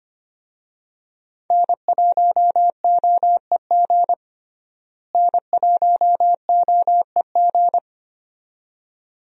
Morse code, heard as a keyed tone, about 25 wpm, 700 Hz. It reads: N1OEG N1OEG
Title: USA Call Signs - 5 Characters - Repeated 2x - 25wpm